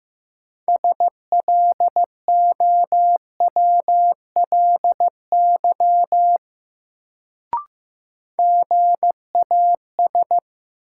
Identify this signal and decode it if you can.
Morse code, heard as a keyed tone, about 15 wpm, 700 Hz. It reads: SLOWLY GAS
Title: Sentences from Top 1100 Words - New Word Drill - 15wpm